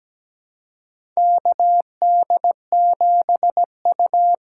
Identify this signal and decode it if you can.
Morse code, heard as a keyed tone, about 17 wpm, 700 Hz. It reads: KD7U
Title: USA Call Signs - 6 Characters - 6x Character Spacing - 17wpm